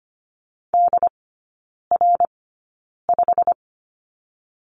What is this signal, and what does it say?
Morse code, heard as a keyed tone, about 25 wpm, spaced as for 9 wpm, 690 Hz. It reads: DR5
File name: Lesson 15e - Number 5 - Three Characters 25wpm